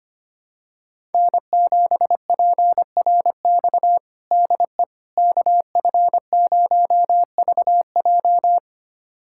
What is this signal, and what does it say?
Morse code, heard as a keyed tone, about 25 wpm, 700 Hz. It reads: N7PRX DE KF0VJ
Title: USA Callsign DE Callsign - 5 Characters - 25wpm